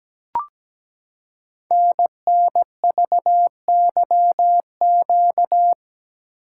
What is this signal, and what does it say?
Morse code, heard as a keyed tone, about 17 wpm, 700 Hz. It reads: NNVYQ
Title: Code Groups - 5 Letters - 17wpm